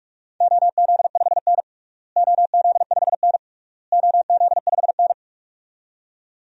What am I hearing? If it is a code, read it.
Morse code, heard as a keyed tone, about 46 wpm, 700 Hz. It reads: OZ5N OZ5N OZ5N
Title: DXCC-100 Call Signs - 4 Characters - Repeated 3x - 45wpm